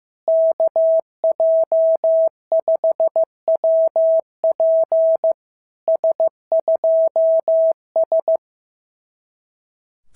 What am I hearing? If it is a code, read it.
Morse code, heard as a keyed tone, about 14 wpm, 650 Hz. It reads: KJ5WP S2S
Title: SOTA QSO Practice 15wpm